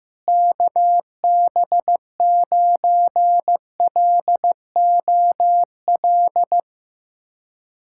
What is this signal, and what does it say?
Morse code, heard as a keyed tone, about 16 wpm, 700 Hz. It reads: KB9LOL
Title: USA Callsign DE Callsign - 6 Characters - 15wpm